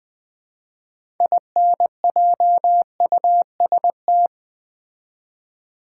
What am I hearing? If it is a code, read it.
Morse code, heard as a keyed tone, about 20 wpm, 700 Hz. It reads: INJUST
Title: Build A Word - 3 Letters At A Time - 9-Letter Words - 20wpm